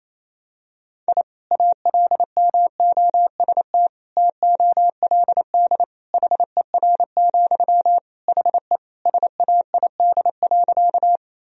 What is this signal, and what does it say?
Morse code, heard as a keyed tone, about 28 wpm, 700 Hz. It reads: I ALMOST TOLD HER, HE SAID.
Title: Sentences from Top 1000 Words - Review Part 4 - 28wpm